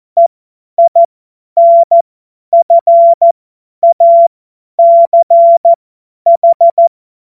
Morse code, heard as a keyed tone, about 14 wpm, spaced as for 10 wpm, 680 Hz. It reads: EINFACH
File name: Morsekurs / Witze / Witz 26. Oktober 2021, 50 F70